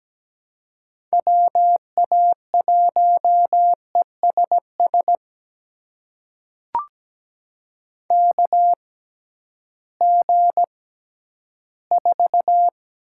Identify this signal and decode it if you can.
Morse code, heard as a keyed tone, about 17 wpm, spaced as for 11 wpm, 700 Hz. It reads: WA1ESS KG4
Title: USA Call Signs - 6 Characters - 6x Character Spacing - 17wpm